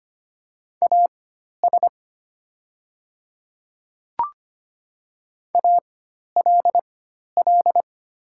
Morse code, heard as a keyed tone, about 25 wpm, spaced as for 12 wpm, 700 Hz. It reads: AS ALL
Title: 4x Character Spacing - Top 100 Words - 25wpm